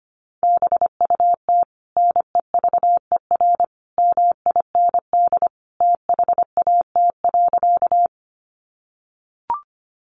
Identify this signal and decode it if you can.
Morse code, heard as a keyed tone, about 25 wpm, 700 Hz. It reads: BUT NEVER MIND THAT.
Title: Sentences from Top 500 Words 25wpm